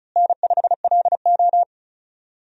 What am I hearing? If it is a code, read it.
Morse code, heard as a keyed tone, about 36 wpm, 700 Hz. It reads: N5LO
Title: USA Call Signs - 4 Characters - Repeated 2x - 35wpm